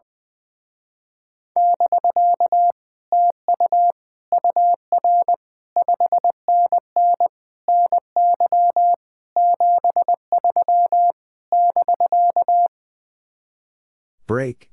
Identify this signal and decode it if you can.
Morse code, heard as a keyed tone, about 20 wpm, 700 Hz. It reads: <BK> TU UR 5NN NY 73 <BK>
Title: Speed-Racing Common QSO Phrases Encore - 20wpm